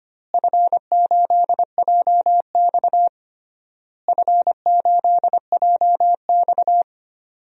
Morse code, heard as a keyed tone, about 25 wpm, 700 Hz. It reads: F8JX F8JX
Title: DXCC-100 Call Signs - 4 Characters - Repeated 3x - 25wpm